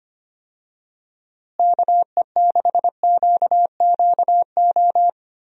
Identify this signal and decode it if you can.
Morse code, heard as a keyed tone, about 25 wpm, 700 Hz. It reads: KE6QQO
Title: USA Call Signs - 6 Characters - Repeated 3x - 25wpm